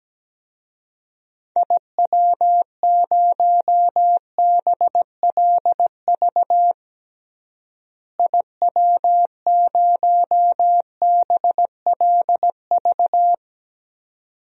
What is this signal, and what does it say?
Morse code, heard as a keyed tone, about 17 wpm, 700 Hz. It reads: IW0BLV IW0BLV
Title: DXCC-100 Call Signs - 6 Characters - Repeated 2x - 17wpm